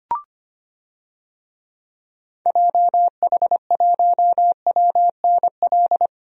Morse code, heard as a keyed tone, about 25 wpm, 700 Hz. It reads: JH1WNL
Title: DXCC-100 Call Signs - 6 Characters - Repeated 3x - 25wpm